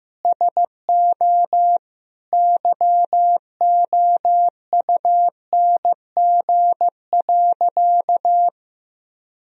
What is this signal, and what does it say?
Morse code, heard as a keyed tone, about 16 wpm, 700 Hz. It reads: SO YOUNG.
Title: Sentences from Top 400 Words and Abbreviations 15wpm